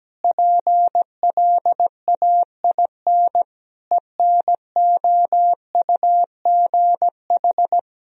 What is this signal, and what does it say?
Morse code, heard as a keyed tone, about 17 wpm, 700 Hz. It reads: PLAIN ENOUGH
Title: Sentences from Top 1000 Words - Review Part 2 - 17wpm